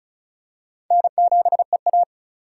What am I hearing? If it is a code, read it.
Morse code, heard as a keyed tone, about 34 wpm, 700 Hz. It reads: N7EA
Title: USA Call Signs - 4 Characters - 3x Character Spacing - 35wpm